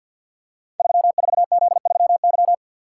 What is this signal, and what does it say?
Morse code, heard as a keyed tone, about 46 wpm, 700 Hz. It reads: 34Z3Y
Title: Code Groups - 5 Letter:Numbers - 50wpm